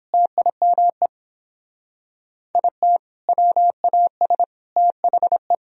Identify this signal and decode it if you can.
Morse code, heard as a keyed tone, about 31 wpm, 700 Hz. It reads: TIME IT WAS THE